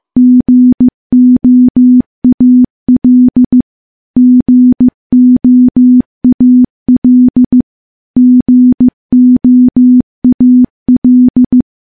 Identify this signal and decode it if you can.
Morse code, heard as a keyed tone, about 15 wpm, 250 Hz. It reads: GOAL GOAL GOAL
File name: morse-training-15WPM